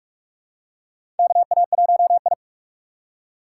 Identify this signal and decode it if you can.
Morse code, heard as a keyed tone, about 46 wpm, 700 Hz. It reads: KA1I
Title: USA Call Signs - 4 Characters - 4x Character Spacing - 45wpm